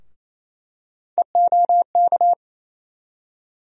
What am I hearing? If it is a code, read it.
Morse code, heard as a keyed tone, about 28 wpm, 700 Hz. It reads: EOK
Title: USA Call Sign Postfixes - 3 Letters 28wpm